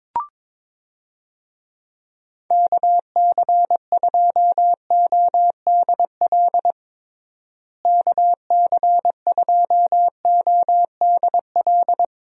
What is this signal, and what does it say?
Morse code, heard as a keyed tone, about 22 wpm, 700 Hz. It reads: KC2ODL KC2ODL
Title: USA Call Signs - 6 Characters - Repeated 2x - 22wpm